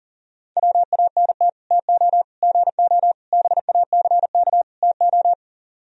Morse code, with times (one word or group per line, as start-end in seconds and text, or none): 0.6-1.5 WANT
1.7-2.2 TO
2.4-3.1 GO
3.3-4.6 BACK
4.8-5.3 TO